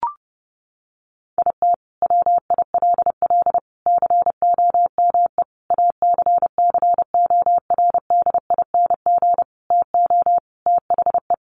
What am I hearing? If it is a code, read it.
Morse code, heard as a keyed tone, about 30 wpm, 700 Hz. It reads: IT WILL COME ACCORDING TO THE
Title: Sentences from Top 1400 Words 30wpm